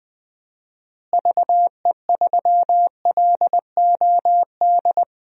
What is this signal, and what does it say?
Morse code, heard as a keyed tone, about 20 wpm, 700 Hz. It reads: VE3LOD
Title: DXCC-100 Call Signs - 6 Characters - Repeated 2x - 20wpm